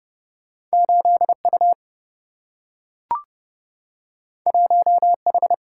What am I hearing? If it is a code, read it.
Morse code, heard as a keyed tone, about 30 wpm, 700 Hz. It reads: 8U 1H
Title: Single-Letter Single-Number 30wpm